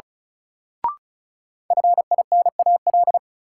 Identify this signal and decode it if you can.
Morse code, heard as a keyed tone, about 34 wpm, 700 Hz. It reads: FINAL
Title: Sentences from Top 800 Words - New Word Drill - 35wpm